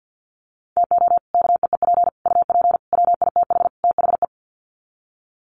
Morse code, heard as a keyed tone, about 50 wpm, 700 Hz. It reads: TO KEEP UP WITH THE
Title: Sets of 5 Words Encore 50wpm